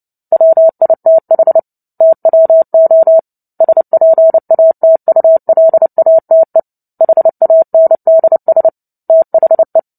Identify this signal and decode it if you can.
Morse code, heard as a keyed tone, about 30 wpm, 650 Hz. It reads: WITH TWO SPATULATE HANDS THE
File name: 29 WPM